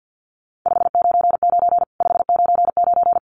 Morse code, heard as a keyed tone, about 50 wpm, 700 Hz. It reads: 599 599
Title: Common QSO Phrases 50wpm